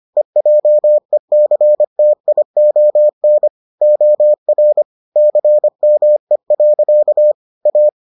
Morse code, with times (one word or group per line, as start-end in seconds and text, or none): 0.2-3.5 EJECTION
3.8-4.8 OR
5.2-7.3 CME.
7.7-7.9 A